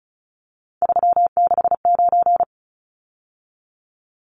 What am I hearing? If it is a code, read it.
Morse code, heard as a keyed tone, about 34 wpm, 700 Hz. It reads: 369
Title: Numbers - 3-Digit 35wpm